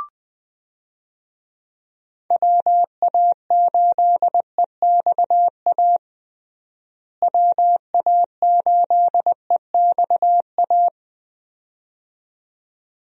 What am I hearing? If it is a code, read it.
Morse code, heard as a keyed tone, about 20 wpm, 700 Hz. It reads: WA8EXA WA8EXA
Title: USA Call Signs - 6 Characters - Repeated 2x - 20wpm